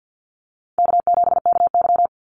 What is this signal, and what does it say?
Morse code, heard as a keyed tone, about 50 wpm, 700 Hz. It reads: K7KY
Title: Easy Call Signs Repeated 2x - 50wpm